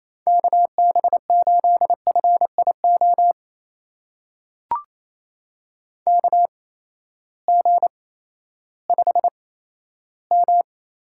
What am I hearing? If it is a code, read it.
Morse code, heard as a keyed tone, about 27 wpm, spaced as for 14 wpm, 700 Hz. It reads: KB8FIO KG5M
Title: USA Call Signs - 6 Characters - 8x Character Spacing - 28wpm